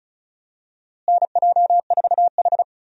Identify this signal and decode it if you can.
Morse code, heard as a keyed tone, about 36 wpm, 700 Hz. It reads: NJ4H
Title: USA Call Signs - 4 Characters - Repeated 2x - 35wpm